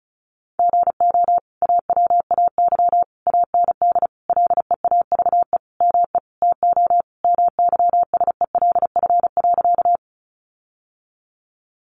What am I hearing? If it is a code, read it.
Morse code, heard as a keyed tone, about 34 wpm, 700 Hz. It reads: GO AWAY AND LEAVE ME TO MYSELF.
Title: Sentences from Top 1300 Words 35wpm